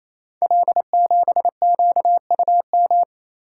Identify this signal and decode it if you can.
Morse code, heard as a keyed tone, about 27 wpm, 700 Hz. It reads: L7QUM
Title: Code Groups - 5 Letter:Numbers - 28wpm